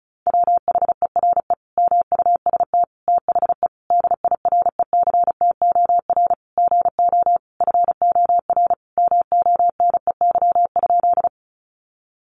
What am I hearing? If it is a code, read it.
Morse code, heard as a keyed tone, about 34 wpm, 700 Hz. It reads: WHERE MUST THE DIRECTOR GO FOR MONEY?
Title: Sentences from Top 1300 Words 35wpm